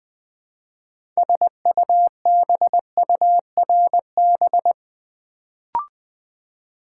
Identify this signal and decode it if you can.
Morse code, heard as a keyed tone, about 20 wpm, 700 Hz. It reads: SUBURB
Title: Build A Word - 2 Letters At A Time - 6-Letter Words - 20wpm